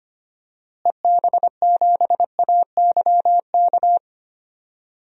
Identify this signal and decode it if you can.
Morse code, heard as a keyed tone, about 25 wpm, 700 Hz. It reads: EB7AYK
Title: DXCC-100 Call Signs - 6 Characters - Repeated 3x - 25wpm